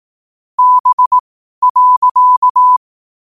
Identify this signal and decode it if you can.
Morse code, heard as a keyed tone, about 18 wpm, spaced as for 12 wpm, 1000 Hz. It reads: B.
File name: qotdfiles / 12wpm